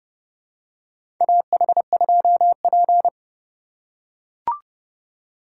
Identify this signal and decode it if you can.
Morse code, heard as a keyed tone, about 31 wpm, 700 Hz. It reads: AH2P
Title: USA Call Signs - 4 Characters - 2x Character Spacing - 30wpm